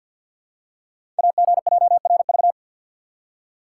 Morse code, heard as a keyed tone, about 52 wpm, 690 Hz. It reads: AGJRV
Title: Code Groups - 5 Letter:Numbers - 50wpm